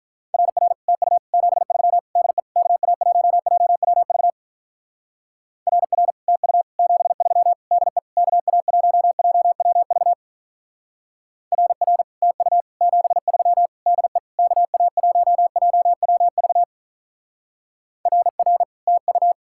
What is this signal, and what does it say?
Morse code, heard as a keyed tone, about 44 wpm, 700 Hz. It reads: RR TU 73 DE KA1JWV RR TU 73 DE KA1JWV RR TU 73 DE KA1JWV RR TU
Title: Speed-Racing Common QSO Phrases - 35wpm